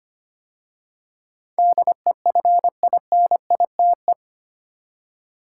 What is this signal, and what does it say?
Morse code, heard as a keyed tone, about 25 wpm, 690 Hz. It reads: DEFINITE
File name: 8 Letter Words - First 3 Letters Given - 25wpm